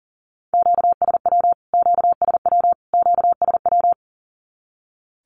Morse code, heard as a keyed tone, about 40 wpm, 700 Hz. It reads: QSW QSW QSW
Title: Q-Codes Repeated 3x 40wpm